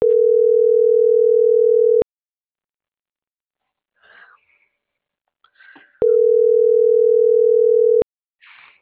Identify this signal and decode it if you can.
Ringback tone (North America), 440 + 480 Hz.